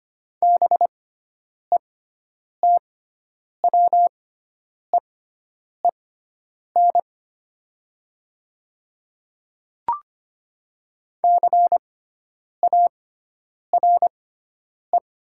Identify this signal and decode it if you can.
Morse code, heard as a keyed tone, about 25 wpm, spaced as for 9 wpm, 700 Hz. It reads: BETWEEN CARE
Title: 6x Character Spacing - Top 300 Words - 25wpm